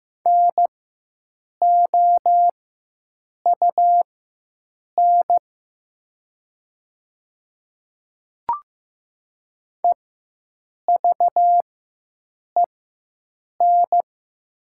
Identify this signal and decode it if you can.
Morse code, heard as a keyed tone, about 15 wpm, spaced as for 7 wpm, 700 Hz. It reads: NOUN EVEN
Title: 4x Character Spacing - Top 500 Words - 15wpm